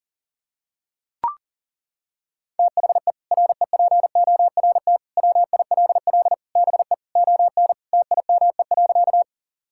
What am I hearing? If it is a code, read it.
Morse code, heard as a keyed tone, about 40 wpm, 700 Hz. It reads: THE REPORT WILL BE ON TIME.
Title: Sentences from Top 1000 Words - Review Part 6 - 40wpm